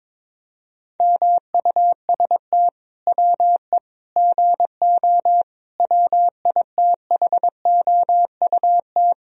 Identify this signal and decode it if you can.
Morse code, heard as a keyed tone, about 22 wpm, 700 Hz. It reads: MUST WE GO WITHOUT
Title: Sentences from Top 1000 Words - Review Part 3 - 22wpm